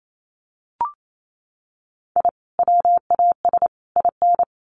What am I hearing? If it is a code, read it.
Morse code, heard as a keyed tone, about 28 wpm, 700 Hz. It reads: I WAS IN
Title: Sets of 3 Words 28wpm